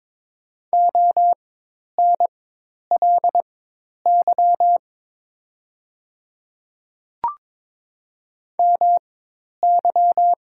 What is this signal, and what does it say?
Morse code, heard as a keyed tone, about 22 wpm, spaced as for 10 wpm, 700 Hz. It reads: ONLY MY
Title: 4x Character Spacing - Top 200 Words - 22wpm